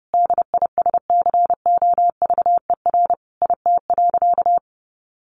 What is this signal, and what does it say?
Morse code, heard as a keyed tone, about 30 wpm, 700 Hz. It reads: DISCOVER IT.
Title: Sentences from Top 1100 Words 30wpm